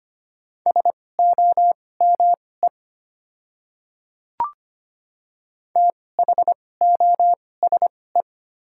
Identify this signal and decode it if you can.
Morse code, heard as a keyed tone, about 25 wpm, spaced as for 18 wpm, 690 Hz. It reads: SOME THOSE
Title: 2x Character Spacing - Top 300 Words - 25wpm